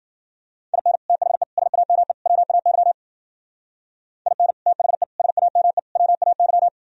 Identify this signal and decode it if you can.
Morse code, heard as a keyed tone, about 50 wpm, 700 Hz. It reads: IN THE SAME WAY IN THE SAME WAY
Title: Speed-Racing - Sets of 4 Words 40wpm